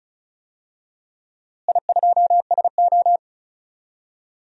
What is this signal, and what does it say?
Morse code, heard as a keyed tone, about 34 wpm, 700 Hz. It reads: I2SO
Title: DXCC-100 Call Signs - 4 Characters - Repeated 3x - 35wpm